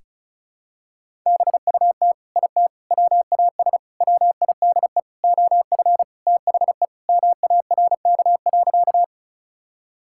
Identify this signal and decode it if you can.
Morse code, heard as a keyed tone, about 34 wpm, 700 Hz. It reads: BUT IT WAS WIDE OF THE MARK.
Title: Sentences from Top 1000 Words - Review Part 6 - 35wpm